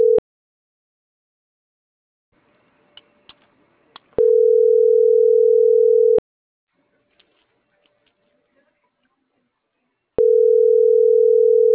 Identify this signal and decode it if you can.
Ringback tone (North America), 440 + 480 Hz.